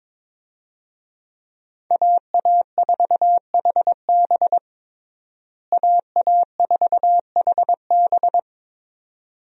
Morse code, heard as a keyed tone, about 22 wpm, 700 Hz. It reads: AA4HB AA4HB
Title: USA Call Signs - 5 Characters - Repeated 2x - 22wpm